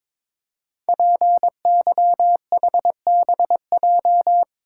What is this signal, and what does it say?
Morse code, heard as a keyed tone, about 22 wpm, 700 Hz. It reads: PYHBJ
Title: Code Groups - 5 Letters - 22wpm